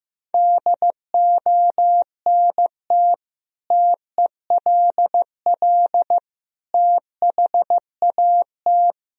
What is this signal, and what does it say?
Morse code, heard as a keyed tone, about 15 wpm, 700 Hz. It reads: DONT TELL THAT
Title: Sentences from Top 1000 Words - Review Part 1 - 15wpm